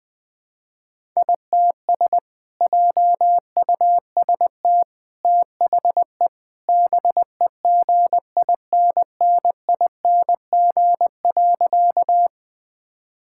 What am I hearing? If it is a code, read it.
Morse code, heard as a keyed tone, about 20 wpm, 700 Hz. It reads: ITS JUST THE BEGINNING.
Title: Sentences from Top 1000 Words - Review Part 5 - 20wpm